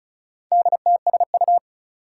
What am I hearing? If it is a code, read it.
Morse code, heard as a keyed tone, about 36 wpm, 700 Hz. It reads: DTSU
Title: Four Challenging Letter-Number Combinations 35wpm